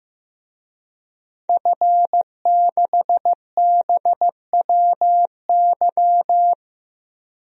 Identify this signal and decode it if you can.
Morse code, heard as a keyed tone, about 15 wpm, 700 Hz. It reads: F6BWY
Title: DXCC-100 Call Signs - 5 Characters - 15wpm